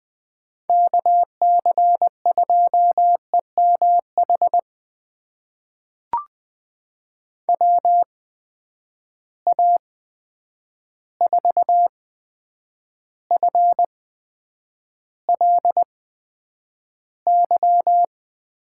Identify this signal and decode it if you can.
Morse code, heard as a keyed tone, about 20 wpm, spaced as for 9 wpm, 700 Hz. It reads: KC2EMH WA4FLY
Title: USA Call Signs - 6 Characters - 8x Character Spacing - 20wpm